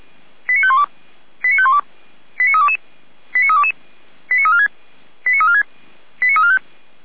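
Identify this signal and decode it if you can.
Five-tone selective call (ZVEI), heard as quick runs of five tones, tones of about 70 ms. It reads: 89512 89512 89322 89322 89346 89346 89345
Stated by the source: ZVEI five-tone call received by SDR, from the audio server of Feuerwehr Orlamünde (f85360).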